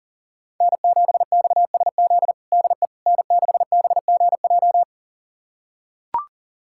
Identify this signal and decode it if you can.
Morse code, heard as a keyed tone, about 39 wpm, 700 Hz. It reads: N7XSZ DE N6BGJ
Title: USA Callsign DE Callsign - 5 Characters - 40wpm